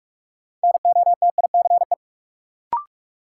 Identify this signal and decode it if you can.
Morse code, heard as a keyed tone, about 46 wpm, 700 Hz. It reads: NOTICE
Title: Sentences from Top 400 Words - New Word Drill - 45wpm